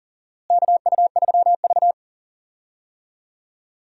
Morse code, heard as a keyed tone, about 40 wpm, 700 Hz. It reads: KU3V
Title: USA Call Signs - 4 Characters - Repeated 2x - 40wpm